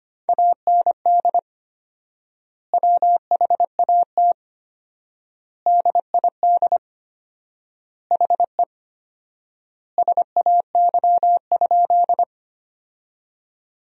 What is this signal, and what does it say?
Morse code, heard as a keyed tone, about 25 wpm, 700 Hz. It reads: AND WHAT DID HE SAY?
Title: 4x Word-Spacing - Sentences from Top 200 Words - 25wpm